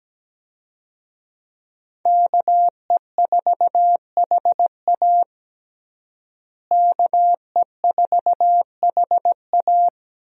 Morse code, heard as a keyed tone, about 17 wpm, 700 Hz. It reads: KE4HA KE4HA
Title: USA Call Signs - 5 Characters - Repeated 3x - 17wpm